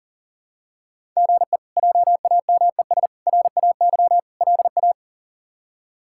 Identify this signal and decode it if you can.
Morse code, heard as a keyed tone, about 40 wpm, 700 Hz. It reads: GE JAMES RAY LA
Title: K1USN SST QSO Practice - 40wpm